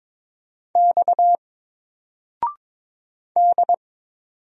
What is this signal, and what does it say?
Morse code, heard as a keyed tone, about 22 wpm, 700 Hz. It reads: X D
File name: Single-Letter-Number-22wpm